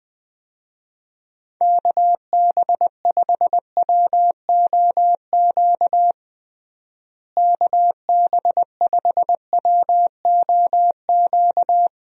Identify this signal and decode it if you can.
Morse code, heard as a keyed tone, about 20 wpm, 700 Hz. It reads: KB5WOQ KB5WOQ
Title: USA Call Signs - 6 Characters - Repeated 3x - 20wpm